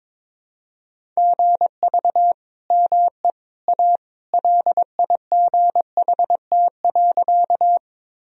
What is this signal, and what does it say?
Morse code, heard as a keyed tone, about 22 wpm, 700 Hz. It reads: GV ME A LIGHT.
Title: Sentences from Top 200 Words and Abbreviations 22wpm